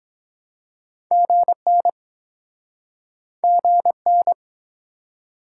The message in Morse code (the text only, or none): GN GN